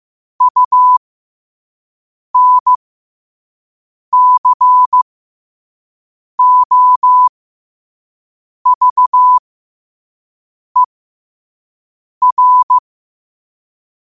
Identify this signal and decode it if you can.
Morse code, heard as a keyed tone, about 15 wpm, spaced as for 5 wpm, 1000 Hz. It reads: UNCOVER
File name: qotdfiles / 5wpm